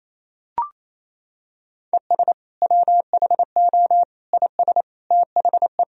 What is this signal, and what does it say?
Morse code, heard as a keyed tone, about 28 wpm, 700 Hz. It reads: ES WHO IS THE
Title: Sentences from Top 200 Words and Abbreviations 28wpm